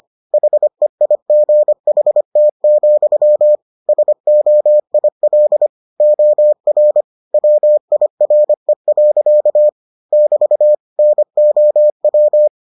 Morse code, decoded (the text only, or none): HEIGHT, SOIL OR WIRE. <BT> NOW